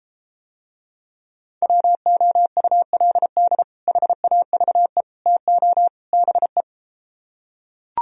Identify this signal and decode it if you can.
Morse code, heard as a keyed tone, about 34 wpm, 700 Hz. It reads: WOULD HAVE TO BE